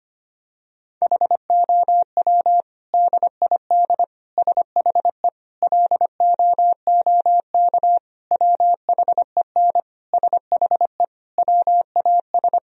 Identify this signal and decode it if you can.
Morse code, heard as a keyed tone, about 25 wpm, 700 Hz. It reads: HOW DID SHE LOOK WHEN SHE WAS
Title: Sentences from Top 1000 Words - Review Part 3 - 25wpm